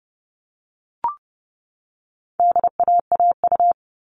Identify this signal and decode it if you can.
Morse code, heard as a keyed tone, about 30 wpm, 700 Hz. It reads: DAAU
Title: All Cut Numbers - 4-Digit 30wpm